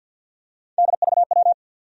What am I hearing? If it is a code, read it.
Morse code, heard as a keyed tone, about 52 wpm, 700 Hz. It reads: DVW
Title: USA Call Sign Postfixes - 3 Letters 50wpm